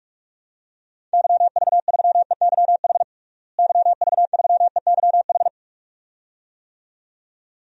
Morse code, heard as a keyed tone, about 46 wpm, 700 Hz. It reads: YV3EYH YV3EYH
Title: DXCC-100 Call Signs - 6 Characters - Repeated 2x - 45wpm